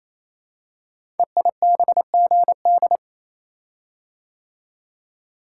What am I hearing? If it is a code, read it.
Morse code, heard as a keyed tone, about 29 wpm, 700 Hz. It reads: EIBGD
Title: Code Groups - 5 Letters - 28wpm